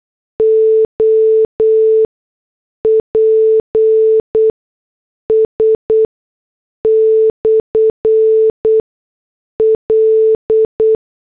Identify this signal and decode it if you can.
Morse code, heard as a keyed tone, about 8 wpm, spaced as for 6 wpm, 440 Hz. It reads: OPS/L